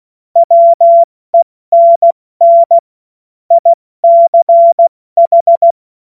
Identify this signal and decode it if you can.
Morse code, heard as a keyed tone, about 16 wpm, spaced as for 14 wpm, 680 Hz. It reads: WENN ICH